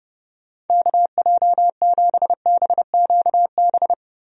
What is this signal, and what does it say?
Morse code, heard as a keyed tone, about 30 wpm, 700 Hz. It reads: KJ7BQB
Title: USA Call Signs - 6 Characters - Repeated 3x - 30wpm